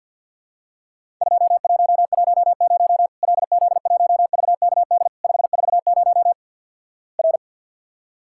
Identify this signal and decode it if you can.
Morse code, heard as a keyed tone, about 50 wpm, 700 Hz. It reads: 2110 LZ1VKD 540 R
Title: Worked All Europe DX Contest - QTC Exchange - 50wpm